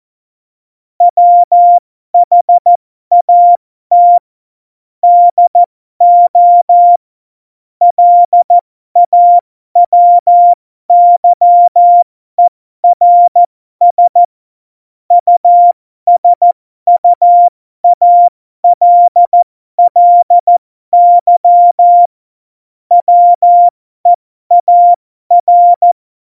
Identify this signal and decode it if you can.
Morse code, heard as a keyed tone, about 14 wpm, spaced as for 12 wpm, 700 Hz. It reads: WHAT DO LAWYERS USUALLY WEAR
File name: Morsekurs / Witze